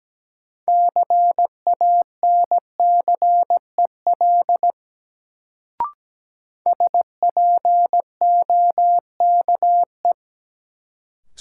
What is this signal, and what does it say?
Morse code, heard as a keyed tone, about 17 wpm, 700 Hz. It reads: CANCEL SPOKE
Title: Sentences from Top 1200 Words - New Word Drill - 17wpm